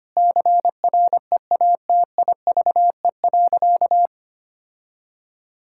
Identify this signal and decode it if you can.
Morse code, heard as a keyed tone, about 25 wpm, 700 Hz. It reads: CREATIVE.
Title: Sentences from Top 1000 Words - Review Part 5 - 25wpm